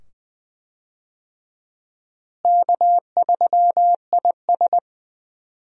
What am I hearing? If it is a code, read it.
Morse code, heard as a keyed tone, about 20 wpm, 700 Hz. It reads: K3IS
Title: USA Call Signs - 4 Characters - Repeated 2x - 20wpm